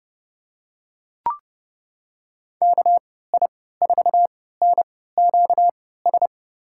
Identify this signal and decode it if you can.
Morse code, heard as a keyed tone, about 31 wpm, spaced as for 17 wpm, 700 Hz. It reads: KI4NQS
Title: USA Call Signs - 6 Characters - 3x Character Spacing - 30wpm